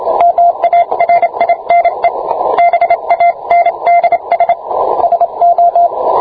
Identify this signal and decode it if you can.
Morse code, heard as a keyed tone, about 26 wpm, 690 Hz, with radio noise. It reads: MARINE BANDS SO